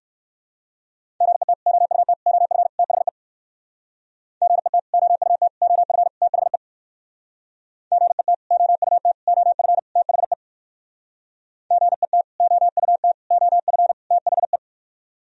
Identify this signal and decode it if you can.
Morse code, heard as a keyed tone, about 58 wpm, 700 Hz. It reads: GET OUT OF THE GET OUT OF THE GET OUT OF THE GET OUT OF THE